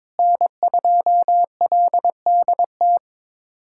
Morse code, heard as a keyed tone, about 22 wpm, 700 Hz. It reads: N2LDT
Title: USA Callsign DE Callsign - 5 Characters - 22wpm